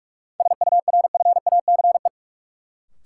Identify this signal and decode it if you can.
Morse code, heard as a keyed tone, about 46 wpm, 700 Hz. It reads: SURFACE
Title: Sets of 4 Words - New Word Drill - 45wpm